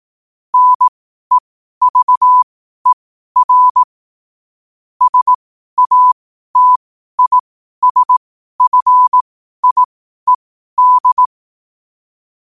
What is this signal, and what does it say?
Morse code, heard as a keyed tone, about 18 wpm, spaced as for 12 wpm, 1000 Hz. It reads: NEVER SATISFIED
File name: qotdfiles / 12wpm